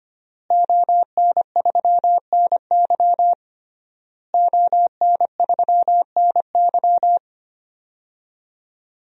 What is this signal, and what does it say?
Morse code, heard as a keyed tone, about 25 wpm, 700 Hz. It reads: ON3NY ON3NY
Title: DXCC-100 Call Signs - 5 Characters - Repeated 3x - 25wpm